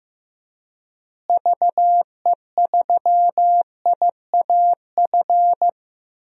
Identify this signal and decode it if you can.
Morse code, heard as a keyed tone, about 15 wpm, 700 Hz. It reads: VE3IAF
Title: DXCC-100 Call Signs - 6 Characters - Repeated 2x - 15wpm